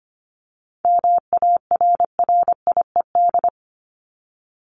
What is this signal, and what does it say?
Morse code, heard as a keyed tone, about 25 wpm, 700 Hz. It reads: MARRIED
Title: Top 1501-1750 Words 25wpm